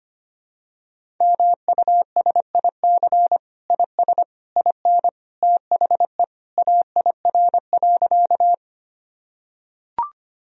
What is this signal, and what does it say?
Morse code, heard as a keyed tone, about 25 wpm, 700 Hz. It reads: MUSIC IS IN THE AIR.
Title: Sentences from Top 1000 Words - Review Part 1 - 25wpm